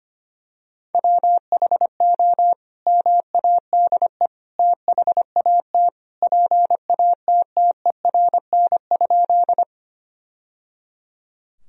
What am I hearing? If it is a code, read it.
Morse code, heard as a keyed tone, about 25 wpm, 700 Hz. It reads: WHO MADE THAT PATTERN?
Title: Sentences from Top 1000 Words - Review Part 2 - 25wpm